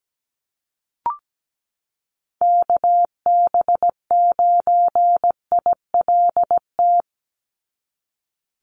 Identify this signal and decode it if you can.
Morse code, heard as a keyed tone, about 17 wpm, 700 Hz. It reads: KB9ILT
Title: NA SOTA Activator Callsigns 17wpm